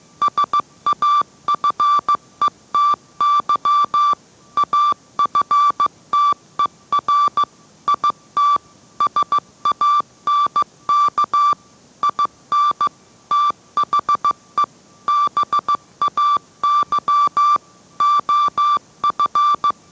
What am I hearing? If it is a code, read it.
Morse code, heard as a keyed tone, about 15 wpm, 1170 Hz. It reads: SAFETY AFTER IT SANK IN THE BAY OF